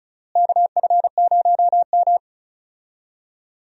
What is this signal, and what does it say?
Morse code, heard as a keyed tone, about 34 wpm, 700 Hz. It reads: KF0M
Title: USA Call Signs - 4 Characters - Repeated 2x - 35wpm